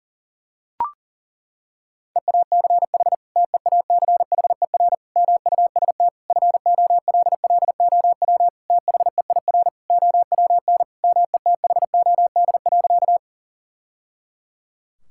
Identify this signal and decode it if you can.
Morse code, heard as a keyed tone, about 37 wpm, 700 Hz. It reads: EACH TEACHER MUST FOLLOW THEIR OWN METHOD.
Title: Sentences from Top 1000 Words - Review Part 6 - 40wpm